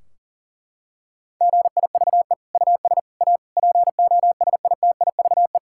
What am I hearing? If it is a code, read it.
Morse code, heard as a keyed tone, about 40 wpm, 700 Hz. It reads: GIVE US A POSITIVE
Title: Sentences from Top 1600 Words 40wpm